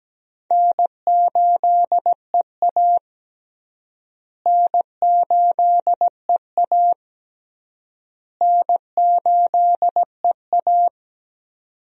Morse code, heard as a keyed tone, about 17 wpm, 700 Hz. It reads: N8EA N8EA N8EA
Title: USA Call Signs - 4 Characters - Repeated 3x - 17wpm